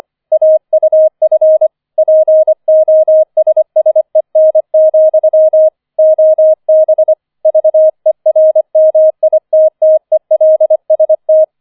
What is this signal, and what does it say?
Morse code, heard as a keyed tone, about 25 wpm, 630 Hz. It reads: AUF POSSEN, OB VERMITTELST
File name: 004 Vierter Streich 25wpm